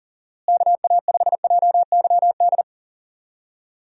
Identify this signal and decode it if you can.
Morse code, heard as a keyed tone, about 40 wpm, 700 Hz. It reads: KA5JYD
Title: USA Call Signs - 6 Characters - Repeated 2x - 40wpm